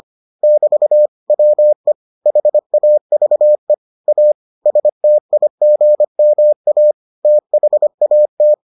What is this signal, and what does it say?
Morse code, heard as a keyed tone, about 25 wpm, 600 Hz. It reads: <BT> WE HAVE A STIGMA THAT